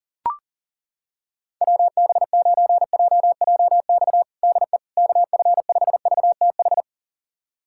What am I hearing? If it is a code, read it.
Morse code, heard as a keyed tone, about 40 wpm, 700 Hz. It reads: WB9JJX DE KF5VTH
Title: USA Callsign DE Callsign - 6 Characters - 40wpm